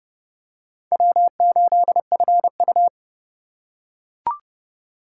Morse code, heard as a keyed tone, about 29 wpm, 700 Hz. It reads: W8FU